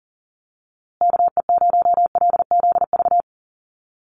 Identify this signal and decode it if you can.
Morse code, heard as a keyed tone, about 38 wpm, 700 Hz. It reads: KE0LZV
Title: USA Call Signs Repeated 2x 40wpm